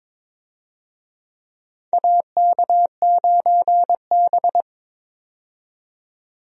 Morse code, heard as a keyed tone, about 22 wpm, 700 Hz. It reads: AK9B
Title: USA Call Signs - 4 Characters - 22wpm